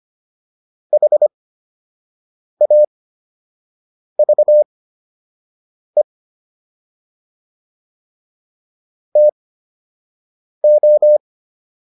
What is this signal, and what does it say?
Morse code, heard as a keyed tone, about 25 wpm, spaced as for 6 wpm, 600 Hz. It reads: HAVE TO